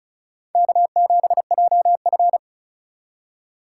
Morse code, heard as a keyed tone, about 36 wpm, 700 Hz. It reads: K7JF
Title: USA Call Signs - 4 Characters - 2x Character Spacing - 35wpm